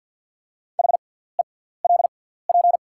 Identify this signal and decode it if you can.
Morse code, heard as a keyed tone, about 48 wpm, spaced as for 17 wpm, 700 Hz. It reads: HELP
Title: 6x Character Spacing - Top 300 Words - 50wpm